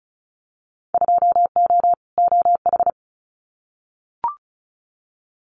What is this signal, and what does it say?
Morse code, heard as a keyed tone, about 36 wpm, 690 Hz. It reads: 2O OH
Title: Winter Field Day Exchanges 35wpm